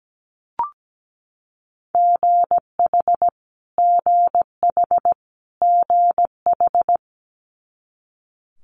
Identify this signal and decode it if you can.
Morse code, heard as a keyed tone, about 17 wpm, 700 Hz. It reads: GH GH GH
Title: Multi-letter Phonemes Repeated 3x 17wpm